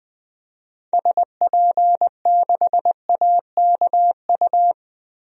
Morse code, heard as a keyed tone, about 20 wpm, 700 Hz. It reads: SP6AKU
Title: DXCC-100 Call Signs - 6 Characters - 20wpm